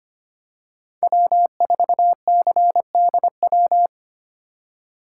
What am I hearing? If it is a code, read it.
Morse code, heard as a keyed tone, about 25 wpm, 700 Hz. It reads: W4CDW
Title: USA Call Signs - 5 Characters - 3x Character Spacing - 25wpm